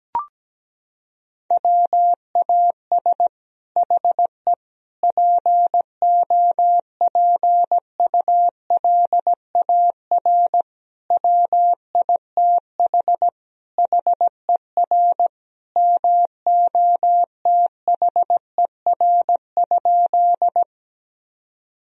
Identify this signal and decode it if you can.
Morse code, heard as a keyed tone, about 17 wpm, 700 Hz. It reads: WAS HE POPULAR WITH HER MOTHER?
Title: Sentences from Top 1000 Words - Review Part 3 - 17wpm